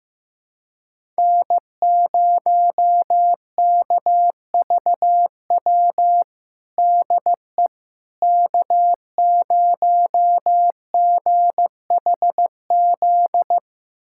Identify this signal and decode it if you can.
Morse code, heard as a keyed tone, about 15 wpm, 700 Hz. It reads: N0KVW DE K0GHZ
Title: USA Callsign DE Callsign - 5 Characters - 15wpm